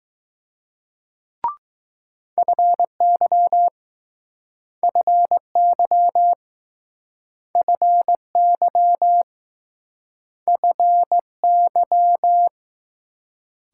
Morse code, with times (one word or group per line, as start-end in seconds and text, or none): 2.4-3.7 FY
4.8-6.3 FY
7.5-9.2 FY
10.5-12.5 FY